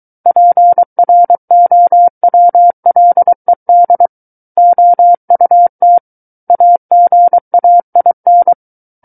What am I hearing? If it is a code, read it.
Morse code, heard as a keyed tone, about 23 wpm, 700 Hz. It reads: PROWLED OUT AGAIN
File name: Book One / 14-Chapter 14 - 23 WPM